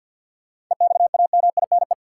Morse code, heard as a keyed tone, about 47 wpm, 700 Hz. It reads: EXAMINE
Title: Sentences from Top 1600 Words - New Word Drill Alt - 50wpm